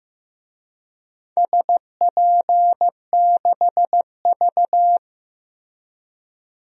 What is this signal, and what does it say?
Morse code, heard as a keyed tone, about 15 wpm, 700 Hz. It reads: SP6V
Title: DXCC-100 Call Signs - 4 Characters - Repeated 3x - 15wpm